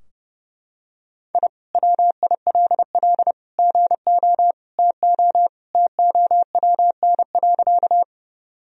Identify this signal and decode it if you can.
Morse code, heard as a keyed tone, about 32 wpm, 700 Hz. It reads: I WILL GO TO TOWN.
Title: Sentences from Top 1000 Words - Review Part 2 - 30wpm